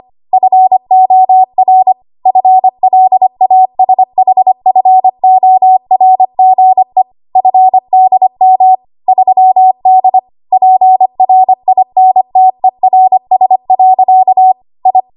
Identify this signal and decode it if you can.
Morse code, heard as a keyed tone, about 25 wpm, 750 Hz. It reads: FOR FLASHFORGE FDM 3D PRINTERS. I
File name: Archive / 25 WPM / 25WPM